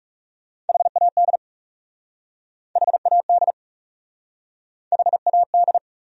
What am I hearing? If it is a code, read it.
Morse code, heard as a keyed tone, about 41 wpm, 700 Hz. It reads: HAD HAD HAD